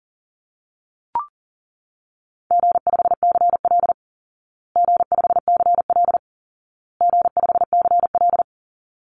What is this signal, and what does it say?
Morse code, heard as a keyed tone, about 40 wpm, 700 Hz. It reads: G5CL G5CL G5CL
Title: Easy Call Signs Repeated 3x - 40wpm